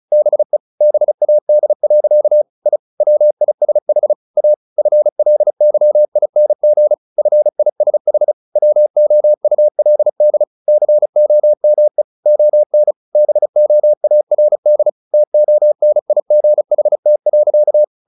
Morse code, heard as a keyed tone, about 34 wpm, 600 Hz. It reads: BE BAD. I WISH A FLYING FISH WOULD COME ON BOARD TONIGHT.
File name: the old man and the sea / the old man and the sea 1 35wpm